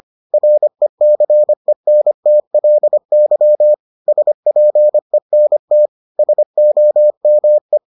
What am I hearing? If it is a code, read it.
Morse code, heard as a keyed tone, about 25 wpm, 600 Hz. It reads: RECENTLY SPENT SOME